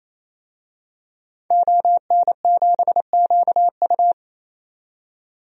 Morse code, heard as a keyed tone, about 27 wpm, 700 Hz. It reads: ON7QU
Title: DXCC-100 Call Signs - 5 Characters - 28wpm